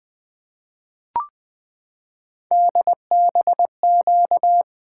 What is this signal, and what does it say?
Morse code, heard as a keyed tone, about 20 wpm, 700 Hz. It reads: DBQ
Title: Three Letter-Number Combinations - 20wpm